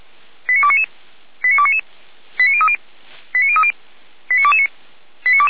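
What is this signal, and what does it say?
Five-tone selective call (ZVEI), heard as quick runs of five tones, tones of about 80 ms. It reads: 89299 89299 89030 89030 89229
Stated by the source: ZVEI five-tone selective call received on an SDR.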